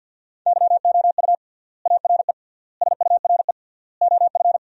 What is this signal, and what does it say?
Morse code, heard as a keyed tone, about 50 wpm, 700 Hz. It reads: YOU ARE SURE OF
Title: 3x Word-Spacing - Sentences from Top 300 Words - 50wpm